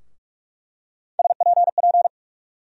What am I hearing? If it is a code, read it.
Morse code, heard as a keyed tone, about 46 wpm, 700 Hz. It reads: SPP
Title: USA Call Sign Postfixes - 3 Letters 45wpm